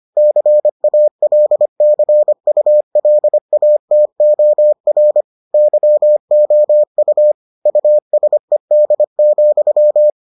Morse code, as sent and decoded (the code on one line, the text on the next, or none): -.-. .- .-.. -.-. ..- .-.. .- - --- .-. / -.-- --- ..- / ..- ... -.. --..--
CALCULATOR YOU USED,